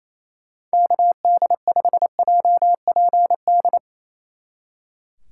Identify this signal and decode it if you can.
Morse code, heard as a keyed tone, about 35 wpm, 700 Hz. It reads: KD5JPD
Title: Speed Racing - USA 2x3 Call Signs 28wpm